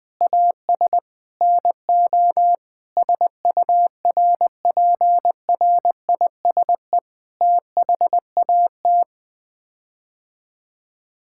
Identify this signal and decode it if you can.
Morse code, heard as a keyed tone, about 20 wpm, 700 Hz. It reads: AS NO SURPRISE THAT
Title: Sets of 8 Words - 20wpm